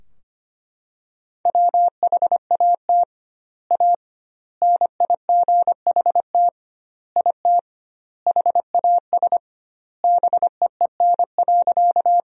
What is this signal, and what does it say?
Morse code, heard as a keyed tone, about 25 wpm, 700 Hz. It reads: WHAT A NIGHT IT HAS BEEN.
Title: 2x Word-Spacing - Sentences from Top 300 Words - 25wpm